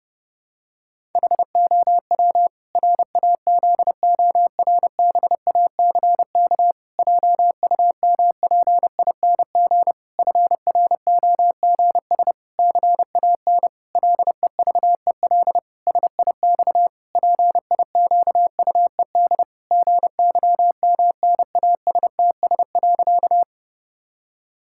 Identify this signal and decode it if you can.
Morse code, heard as a keyed tone, about 32 wpm, 700 Hz. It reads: HOW RAZORBACK JUMPING FROGS CAN LEVEL SIX PIQUED GYMNASTS.
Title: Pangram Challenge v2 - 30wpm